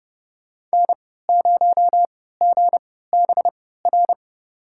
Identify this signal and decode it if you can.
Morse code, heard as a keyed tone, about 32 wpm, spaced as for 17 wpm, 700 Hz. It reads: N0GBR
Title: USA Call Signs - 5 Characters - 3x Character Spacing - 30wpm